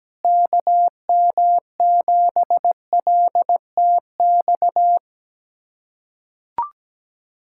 Morse code, heard as a keyed tone, about 17 wpm, 700 Hz. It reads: KM7LTX
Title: USA Callsign DE Callsign 17wpm